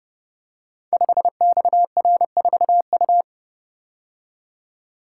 Morse code, heard as a keyed tone, about 30 wpm, 700 Hz. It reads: 5XR4U
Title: Code Groups - 5 Letter:Numbers - 30wpm